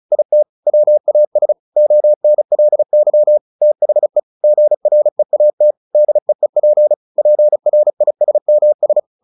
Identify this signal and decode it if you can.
Morse code, heard as a keyed tone, about 34 wpm, 600 Hz. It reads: IT WAS ONLY THE GREAT DEEP PRISMS
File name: the old man and the sea / the old man and the sea 1 35wpm